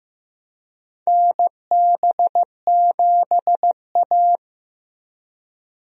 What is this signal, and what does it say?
Morse code, heard as a keyed tone, about 16 wpm, 700 Hz. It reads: NB7A
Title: USA Call Signs - 4 Characters - Repeated 2x - 15wpm